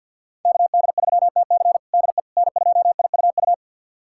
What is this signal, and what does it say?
Morse code, heard as a keyed tone, about 48 wpm, 700 Hz. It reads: KD3TC DE N2IUU
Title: USA Callsign DE Callsign - 5 Characters - 50wpm